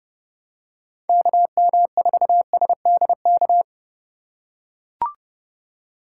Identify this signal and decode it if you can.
Morse code, heard as a keyed tone, about 30 wpm, 700 Hz. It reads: KM4SDK